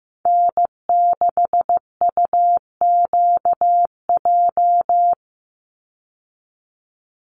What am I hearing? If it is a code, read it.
Morse code, heard as a keyed tone, about 15 wpm, 700 Hz. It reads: N6UQJ
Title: USA Call Signs 15wpm